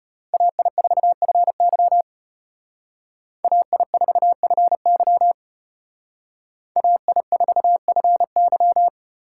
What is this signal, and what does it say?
Morse code, heard as a keyed tone, about 31 wpm, 700 Hz. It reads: AI4FY AI4FY AI4FY
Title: Speed Racing - USA 2x2 Call Signs 25wpm